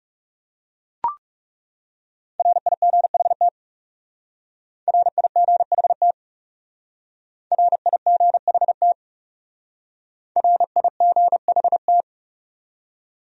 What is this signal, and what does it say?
Morse code, heard as a keyed tone, about 38 wpm, 700 Hz. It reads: RIGHT RIGHT RIGHT RIGHT